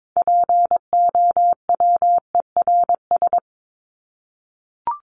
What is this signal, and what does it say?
Morse code, heard as a keyed tone, about 22 wpm, 700 Hz. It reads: POWERS
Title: ES:S Suffix - Top 100 Words with Base Words 22wpm